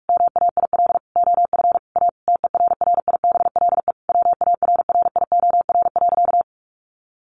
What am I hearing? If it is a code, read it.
Morse code, heard as a keyed tone, about 46 wpm, 700 Hz. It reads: MAIL OF A TERRIBLE WARRIOR.
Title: A Sound Of Thunder 45wpm